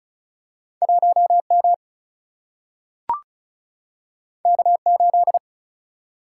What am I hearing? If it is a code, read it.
Morse code, heard as a keyed tone, about 36 wpm, 700 Hz. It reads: 1M K8
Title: Single-Letter Single-Number 35wpm